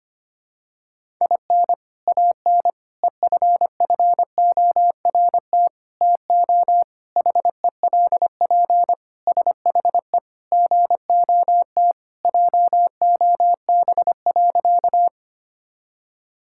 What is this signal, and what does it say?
Morse code, heard as a keyed tone, about 25 wpm, 700 Hz. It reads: IN AN EFFORT TO HELP SHE GOT JOB.
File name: practice / 25wpm / Sentences with 4-Word Phrases - 25wpm